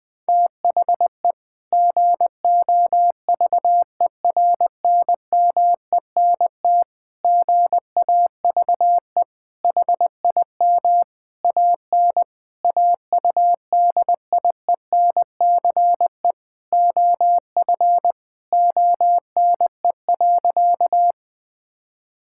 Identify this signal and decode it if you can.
Morse code, heard as a keyed tone, about 20 wpm, 700 Hz. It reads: THE GOVERNMENT GAVE HIM AN AUDIENCE OF ONE.
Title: Sentences from Top 1000 Words - Review Part 5 - 20wpm